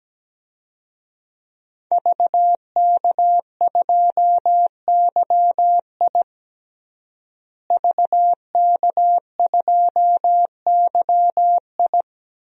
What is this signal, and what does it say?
Morse code, heard as a keyed tone, about 17 wpm, 700 Hz. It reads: VK2YI VK2YI
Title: DXCC-100 Call Signs - 5 Characters - Repeated 2x - 17wpm